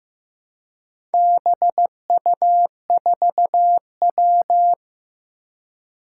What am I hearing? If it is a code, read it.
Morse code, heard as a keyed tone, about 15 wpm, 700 Hz. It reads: BU4W
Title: Four Challenging Letter-Number Combinations 15wpm